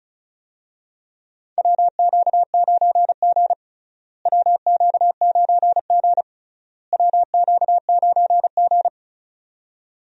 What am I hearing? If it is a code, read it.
Morse code, heard as a keyed tone, about 34 wpm, 700 Hz. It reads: WQ9G WQ9G WQ9G
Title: USA Call Signs - 4 Characters - Repeated 3x - 35wpm